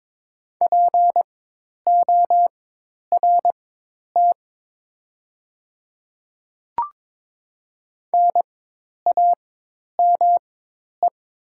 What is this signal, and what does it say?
Morse code, heard as a keyed tone, about 22 wpm, spaced as for 10 wpm, 700 Hz. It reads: PORT NAME
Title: 4x Character Spacing - Top 200 Words - 22wpm